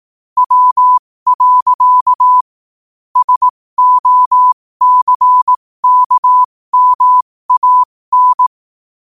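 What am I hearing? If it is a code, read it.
Morse code, heard as a keyed tone, about 18 wpm, spaced as for 15 wpm, 1000 Hz. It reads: W. SOCKMAN
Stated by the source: Morse practice sent at 15 wpm.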